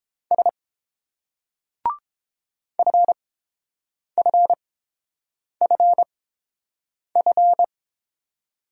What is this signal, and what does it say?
Morse code, heard as a keyed tone, about 30 wpm, spaced as for 7 wpm, 690 Hz. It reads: S FFFF